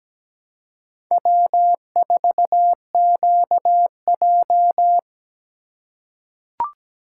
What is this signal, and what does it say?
Morse code, heard as a keyed tone, about 17 wpm, 700 Hz. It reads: W4QJ
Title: USA Call Signs - 3x Character Spacing - 17wpm